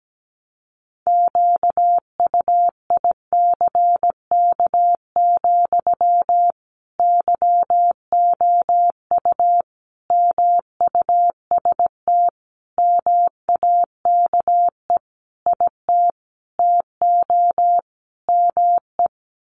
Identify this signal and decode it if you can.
Morse code, heard as a keyed tone, about 17 wpm, 700 Hz. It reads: QUICK, YOU MUST MAKE IT TO ME
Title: Sentences from Top 500 Words 17wpm